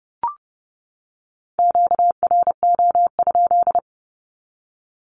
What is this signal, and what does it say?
Morse code, heard as a keyed tone, about 30 wpm, 700 Hz. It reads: QRO?
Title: Q-Codes 30wpm